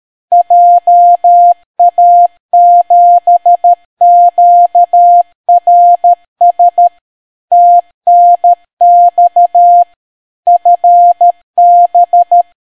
Morse code, heard as a keyed tone, about 13 wpm, 700 Hz. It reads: JA7QRS TNX FB